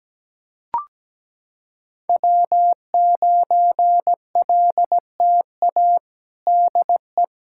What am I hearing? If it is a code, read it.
Morse code, heard as a keyed tone, about 17 wpm, 700 Hz. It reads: W9LTA DE
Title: USA Callsign DE Callsign - 5 Characters - 17wpm